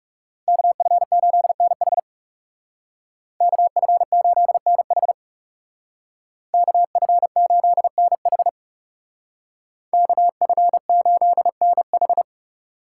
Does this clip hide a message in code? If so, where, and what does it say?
Yes: Morse code from 0.5 to 12.2 s, heard as a keyed tone, about 39 wpm, 700 Hz. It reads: KF8NH KF8NH KF8NH KF8NH